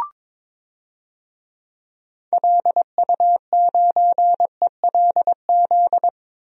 Morse code, heard as a keyed tone, about 22 wpm, 700 Hz. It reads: LU9ELZ
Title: DXCC-100 Call Signs - 6 Characters - 22wpm